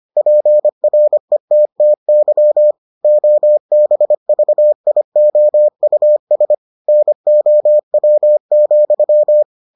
Morse code, heard as a keyed tone, about 25 wpm, 600 Hz. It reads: PRETTY OBVIOUS NOW,